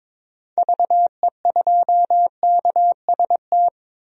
Morse code, heard as a keyed tone, about 22 wpm, 700 Hz. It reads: VE2KST
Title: DXCC-100 Call Signs - 6 Characters - 22wpm